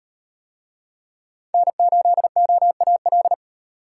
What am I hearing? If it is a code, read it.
Morse code, heard as a keyed tone, about 36 wpm, 700 Hz. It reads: N8OAL